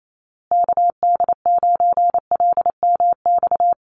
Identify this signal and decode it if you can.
Morse code, heard as a keyed tone, about 28 wpm, 700 Hz. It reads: KD9LMX
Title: USA Call Signs Repeated 3x 28wpm